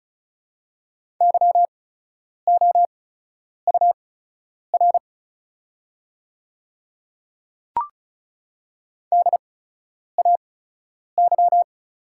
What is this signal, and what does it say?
Morse code, heard as a keyed tone, about 36 wpm, spaced as for 10 wpm, 700 Hz. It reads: YOUR DAY
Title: 8x Character Spacing - Top 100 Words - 35wpm